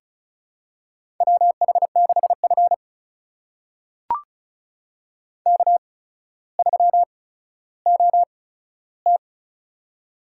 Morse code, heard as a keyed tone, about 36 wpm, spaced as for 15 wpm, 700 Hz. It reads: WH6F K3OT
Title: USA Call Signs - 8x Character Spacing - 35wpm